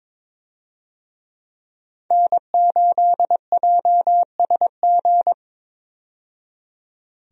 Morse code, heard as a keyed tone, about 22 wpm, 700 Hz. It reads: N8JSG
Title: USA Call Signs - 5 Characters - 22wpm